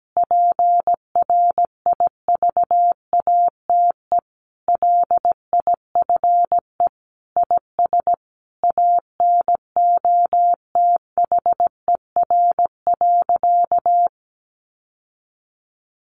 Morse code, heard as a keyed tone, about 17 wpm, 700 Hz. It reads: PRIVATE LIFE IS ANOTHER.
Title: Sentences from Top 1000 Words 17wpm